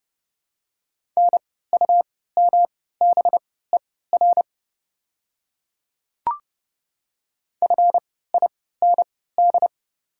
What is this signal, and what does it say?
Morse code, heard as a keyed tone, about 32 wpm, spaced as for 17 wpm, 700 Hz. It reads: NUMBER FIND
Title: 3x Character Spacing - Top 100 Words - 30wpm